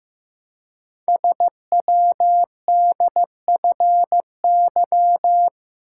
Morse code, heard as a keyed tone, about 15 wpm, 700 Hz. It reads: SWDFY
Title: Code Groups - 5 Letters - 15wpm